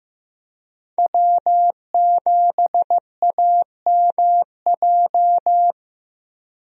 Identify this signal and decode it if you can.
Morse code, heard as a keyed tone, about 15 wpm, 700 Hz. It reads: W7AMJ
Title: USA Call Signs - 5 Characters - 15wpm